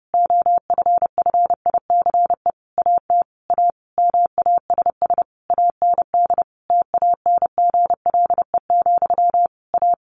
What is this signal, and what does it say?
Morse code, heard as a keyed tone, about 30 wpm, 700 Hz. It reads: OFFICE AT A MASS AND TANGLE, A
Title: A Sound Of Thunder 30wpm